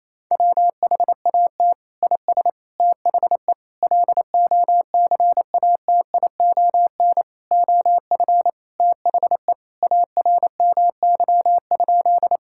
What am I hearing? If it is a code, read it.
Morse code, heard as a keyed tone, about 28 wpm, 700 Hz. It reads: WHAT IS THE LOCATION OF THE ARMY?
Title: Sentences from Top 1000 Words - Review Part 6 - 28wpm